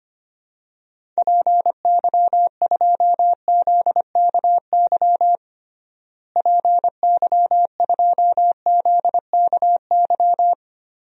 Morse code, heard as a keyed tone, about 25 wpm, 700 Hz. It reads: PY2ZKY PY2ZKY
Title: DXCC-100 Call Signs - 6 Characters - Repeated 3x - 25wpm